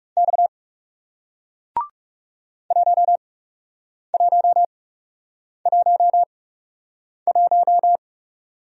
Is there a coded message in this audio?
Yes: Morse code from 0.2 to 8.0 s, heard as a keyed tone, about 40 wpm, spaced as for 8 wpm, 700 Hz. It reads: X 1111